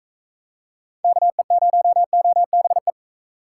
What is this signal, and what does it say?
Morse code, heard as a keyed tone, about 40 wpm, 700 Hz. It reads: KE0OBE